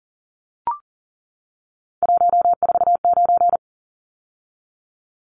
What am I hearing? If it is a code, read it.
Morse code, heard as a keyed tone, about 40 wpm, 700 Hz. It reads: 149
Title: Numbers - 3-Digit 40wpm